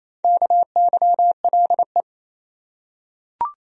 Morse code, heard as a keyed tone, about 28 wpm, 700 Hz. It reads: KYLE
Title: USA Common Names - 28wpm